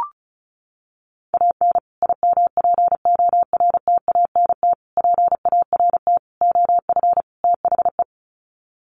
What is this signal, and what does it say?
Morse code, heard as a keyed tone, about 34 wpm, 700 Hz. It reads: AN IMPORTANT PART OF THE
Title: Sets of 5 Words Encore 35wpm